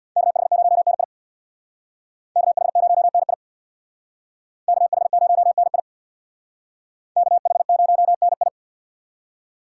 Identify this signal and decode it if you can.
Morse code, heard as a keyed tone, about 63 wpm, 700 Hz. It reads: KH0NI KH0NI KH0NI KH0NI